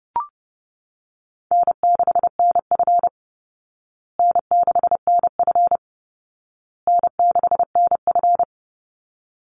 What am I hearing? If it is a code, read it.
Morse code, heard as a keyed tone, about 28 wpm, 700 Hz. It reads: N6NF N6NF N6NF
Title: Call Signs Repeated 2x - 30wpm